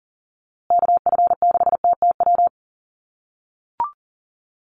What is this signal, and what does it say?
Morse code, heard as a keyed tone, about 37 wpm, 700 Hz. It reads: KF6TTW
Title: USA Call Signs Repeated 3x 40wpm